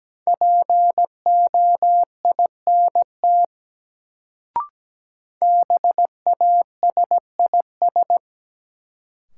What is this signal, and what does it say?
Morse code, heard as a keyed tone, about 17 wpm, 700 Hz. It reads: POINT BASIS
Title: Sets of 3 Words - New Word Drill - 17wpm